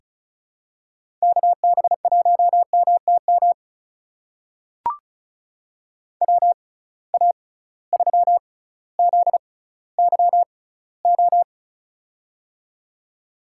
Morse code, heard as a keyed tone, about 36 wpm, spaced as for 18 wpm, 700 Hz. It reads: KB1MTM WA3ZYO
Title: USA Call Signs - 6x Character Spacing - 35wpm